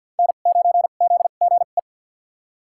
Morse code, heard as a keyed tone, about 53 wpm, spaced as for 37 wpm, 700 Hz. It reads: N9ZGE
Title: USA Call Signs - 5 Characters - 2x Character Spacing - 50wpm